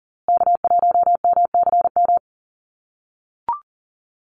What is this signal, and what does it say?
Morse code, heard as a keyed tone, about 40 wpm, 700 Hz. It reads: K1MCM
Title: USA Call Signs Repeated 3x 40wpm